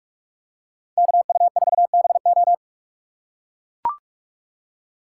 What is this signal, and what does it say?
Morse code, heard as a keyed tone, about 44 wpm, 700 Hz. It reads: KU4BO
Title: USA Call Signs - 5 Characters - 3x Character Spacing - 45wpm